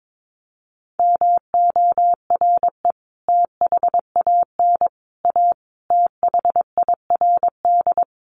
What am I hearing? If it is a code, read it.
Morse code, heard as a keyed tone, about 22 wpm, 700 Hz. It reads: MORE THAN A THIRD
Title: Sets of 5 Words Encore2 22wpm